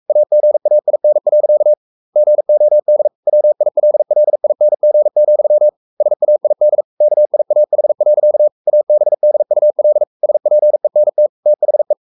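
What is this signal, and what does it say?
Morse code, heard as a keyed tone, about 44 wpm, 600 Hz. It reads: AGAIN. GOD WILLING, SAID KIAH. ABDUL SPENT THE
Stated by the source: Morse code at 43 wpm sending Ken Follett's novel Never in English, from CW med Gnister.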